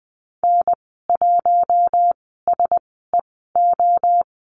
Morse code, heard as a keyed tone, about 20 wpm, spaced as for 14 wpm, 700 Hz. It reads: N1SEO
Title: 042 Lesson 8g - Number 1 - Call Signs 20wpm